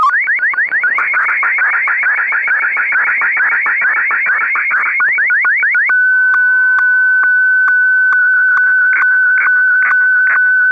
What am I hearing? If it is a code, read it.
SSTV (slow-scan television), Martin 1.